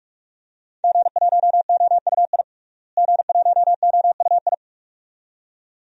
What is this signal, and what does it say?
Morse code, heard as a keyed tone, about 43 wpm, 700 Hz. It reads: G1OUI G1OUI
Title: DXCC-100 Call Signs - 5 Characters - Repeated 2x - 45wpm